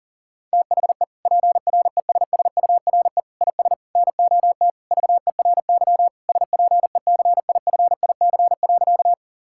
Morse code, heard as a keyed tone, about 40 wpm, 700 Hz. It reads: THE PRESSURE IS NOT VERY SPECIFIC.
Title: Sentences from Top 1000 Words - Review Part 6 - 40wpm